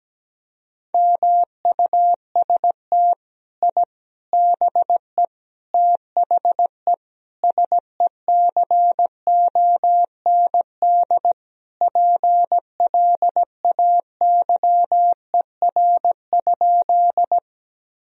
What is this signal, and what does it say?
Morse code, heard as a keyed tone, about 17 wpm, 700 Hz. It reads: MUST I BE THE SECOND PLAYER?
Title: Sentences from Top 1000 Words - Review Part 4 - 17wpm